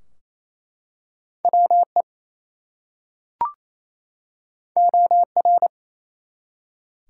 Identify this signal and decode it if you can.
Morse code, heard as a keyed tone, about 27 wpm, 700 Hz. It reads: WE OR
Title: Two Letter Words - 28wpm